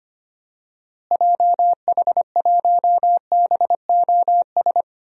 Morse code, heard as a keyed tone, about 25 wpm, 700 Hz. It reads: JH1BOS
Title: DXCC-100 Call Signs - 6 Characters - Repeated 2x - 25wpm